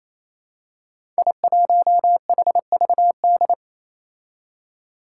Morse code, heard as a keyed tone, about 27 wpm, 700 Hz. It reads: I1HVD
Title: Five Challenging Letter-Number Combinations 28wpm